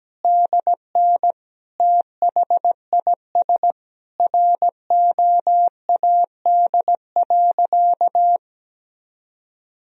Morse code, heard as a keyed tone, about 17 wpm, 700 Hz. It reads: DN THIS ROAD.
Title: Sentences from Top 400 Words and Abbreviations 17wpm